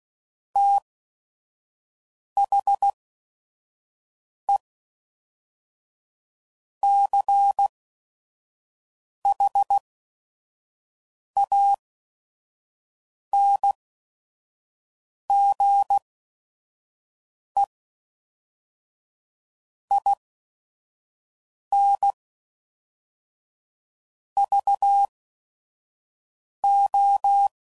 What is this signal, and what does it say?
Morse code, heard as a keyed tone, about 16 wpm, spaced as for 5 wpm, 800 Hz. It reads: THE CHANGE INVO
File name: CW Code Groups @5WPM / Technical Subjects / XC